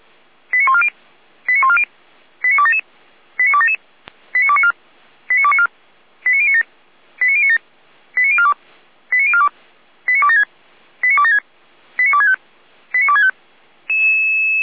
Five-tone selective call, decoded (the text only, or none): 89140 89140 89277 89277 89294 89294 89097 89097 89042 89042 89276 89276 89265 89265